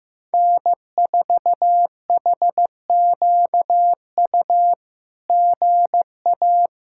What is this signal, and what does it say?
Morse code, heard as a keyed tone, about 15 wpm, 700 Hz. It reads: N4HQU GA
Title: SOTA QSO Practice 15wpm